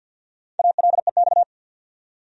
Morse code, heard as a keyed tone, about 47 wpm, 700 Hz. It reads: ALEX
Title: USA Common Names - 50wpm